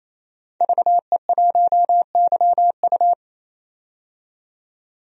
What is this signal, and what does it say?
Morse code, heard as a keyed tone, about 28 wpm, 700 Hz. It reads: VE1YU
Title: DXCC-100 Call Signs - 5 Characters - 28wpm